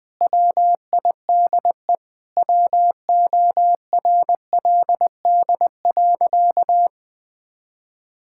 Morse code, heard as a keyed tone, about 19 wpm, 700 Hz. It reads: WIDE WORLD.
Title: Sentences from Top 1000 Words - Review Part 6 - 20wpm